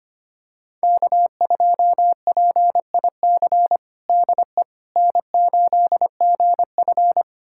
Morse code, heard as a keyed tone, about 25 wpm, 700 Hz. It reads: K2PIC DE N8GF
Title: USA Callsign DE Callsign 25wpm